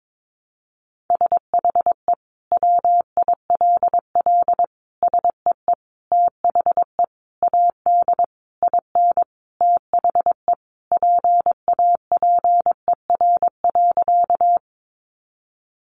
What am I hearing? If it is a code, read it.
Morse code, heard as a keyed tone, about 22 wpm, 700 Hz. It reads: SHE WILL SEE THE AD IN THE PAPER.
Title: Sentences from Top 1300 Words 22wpm